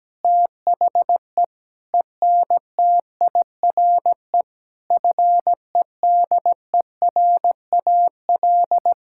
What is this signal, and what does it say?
Morse code, heard as a keyed tone, about 17 wpm, 700 Hz. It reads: THE ENTIRE FEDERAL
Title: Sentences from Top 1000 Words - Review Part 5 - 17wpm